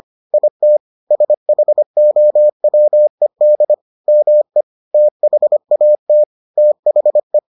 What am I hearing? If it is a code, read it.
Morse code, heard as a keyed tone, about 25 wpm, 600 Hz. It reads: IT SHOWED ME THAT THE